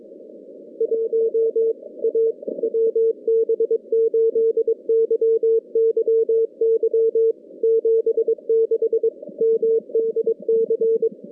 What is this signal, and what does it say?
Morse code, heard as a keyed tone, about 23 wpm, 440 Hz. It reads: 1 A WB8YYY 76 MDC
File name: contests / CW-SS-k5tr- -2100